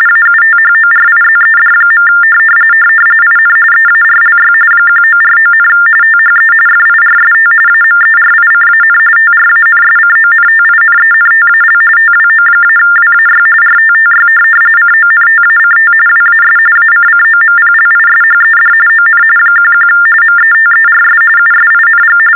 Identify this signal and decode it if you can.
TTY/TDD text telephone tones, about 45.45 baud. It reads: AUTHORIZED. IN EARLY 1949, THE FIRST AMERICAN TRANSCONTINENTAL TWO-WAY RTTY CONTACT WAS ACCOMPLISHED ON 11 METERS USING
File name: teletype-RTTY-45